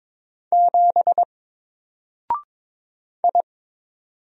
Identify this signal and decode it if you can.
Morse code, heard as a keyed tone, about 22 wpm, 700 Hz. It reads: 7 I
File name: Single-Letter-Number-22wpm